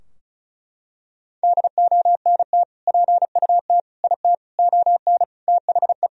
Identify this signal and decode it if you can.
Morse code, heard as a keyed tone, about 36 wpm, 700 Hz. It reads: DONT PUT IT ON THE
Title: Sentences from Top 1000 Words - Review Part 2 - 35wpm